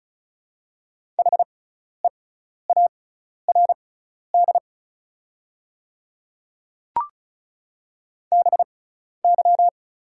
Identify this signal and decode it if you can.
Morse code, heard as a keyed tone, about 36 wpm, spaced as for 12 wpm, 690 Hz. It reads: HEARD BY
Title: 6x Character Spacing - Top 500 Words - 35wpm